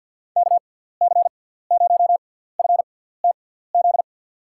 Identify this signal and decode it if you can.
Morse code, heard as a keyed tone, about 51 wpm, spaced as for 17 wpm, 700 Hz. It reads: KC0FTZ
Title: USA Call Signs - 6x Character Spacing - 50wpm